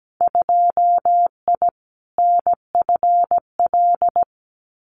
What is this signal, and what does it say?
Morse code, heard as a keyed tone, about 17 wpm, 700 Hz. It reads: 2I NFL
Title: Winter Field Day Exchanges 17wpm